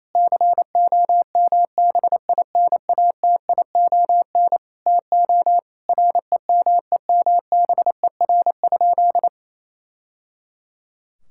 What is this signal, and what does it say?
Morse code, heard as a keyed tone, about 28 wpm, 700 Hz. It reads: COMBINATION TO REMEMBER?
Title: Sentences from Top 1000 Words - Review Part 6 - 28wpm